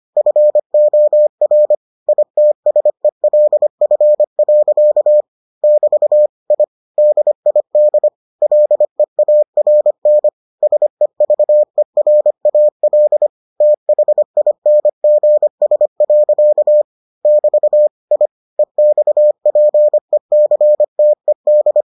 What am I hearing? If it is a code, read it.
Morse code, heard as a keyed tone, about 25 wpm, 600 Hz. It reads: FOR ITSELF. <BT> I DID LEARN SEVERAL THINGS. <BT> I EXPECTED